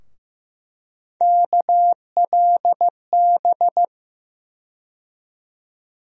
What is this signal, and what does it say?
Morse code, heard as a keyed tone, about 15 wpm, 700 Hz. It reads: KLB
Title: USA Call Sign Postfixes - 3 Letters 15wpm